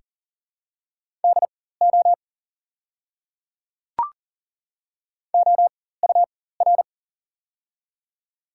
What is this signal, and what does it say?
Morse code, heard as a keyed tone, about 40 wpm, spaced as for 19 wpm, 700 Hz. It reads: DO OUR
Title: 4x Character Spacing - Top 300 Words - 40wpm